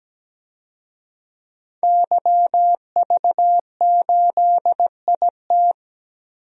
Morse code, heard as a keyed tone, about 17 wpm, 700 Hz. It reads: YV8IT
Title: DXCC-100 Call Signs - 5 Characters - Repeated 3x - 17wpm